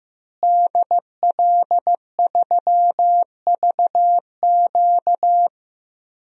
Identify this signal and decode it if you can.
Morse code, heard as a keyed tone, about 15 wpm, 700 Hz. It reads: DL3VQ
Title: DXCC-100 Call Signs - 5 Characters - Repeated 2x - 15wpm